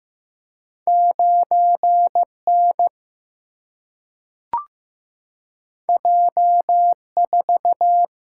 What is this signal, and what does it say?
Morse code, heard as a keyed tone, about 15 wpm, 700 Hz. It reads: 9 N J 4
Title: Single-Letter Single-Number 15wpm